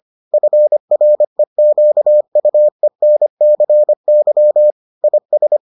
Morse code, heard as a keyed tone, about 25 wpm, 600 Hz. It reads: FREQUENCY IS